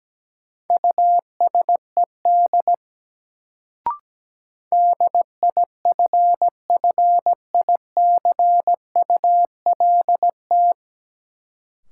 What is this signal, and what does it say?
Morse code, heard as a keyed tone, about 17 wpm, 700 Hz. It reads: USED DIFFICULT
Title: Sentences from Top 600 Words - New Word Drill - 17wpm